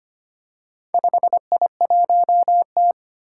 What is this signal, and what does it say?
Morse code, heard as a keyed tone, about 25 wpm, 700 Hz. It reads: 5I1T
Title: Four Challenging Letter-Number Combinations 25wpm